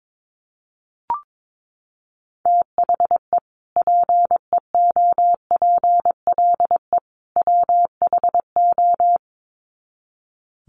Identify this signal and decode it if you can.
Morse code, heard as a keyed tone, about 22 wpm, 700 Hz. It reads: THE PEOPLE WHO
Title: Sets of 3 Words 22wpm